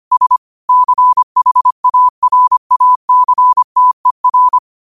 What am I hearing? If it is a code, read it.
Morse code, heard as a keyed tone, about 25 wpm, 1000 Hz. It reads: S CHARACTER